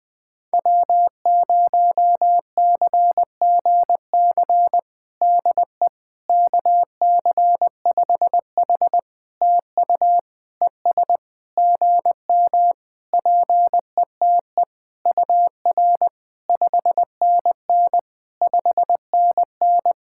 Morse code, heard as a keyed tone, about 20 wpm, 700 Hz. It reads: W0CGC DE KC5H TU ES GM PETE UR 5NN 5NN
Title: POTA QSO Practice - 20wpm